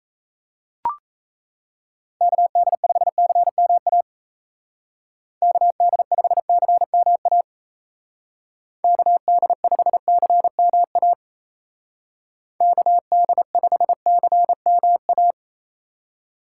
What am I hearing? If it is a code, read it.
Morse code, heard as a keyed tone, about 33 wpm, 700 Hz. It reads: KD5CMA KD5CMA KD5CMA KD5CMA
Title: Speed Racing - USA 2x3 Call Signs 28wpm